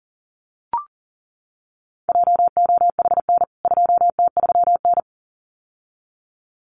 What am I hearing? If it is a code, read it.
Morse code, heard as a keyed tone, about 37 wpm, 700 Hz. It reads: JOHN 2T3N
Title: CWT Exchange 40wpm